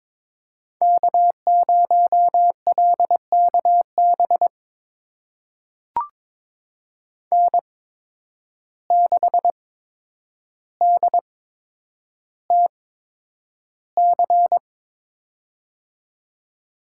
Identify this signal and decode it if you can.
Morse code, heard as a keyed tone, about 21 wpm, spaced as for 9 wpm, 700 Hz. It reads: K0LKB N6DTC
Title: USA Call Signs - 5 Characters - 8x Character Spacing - 22wpm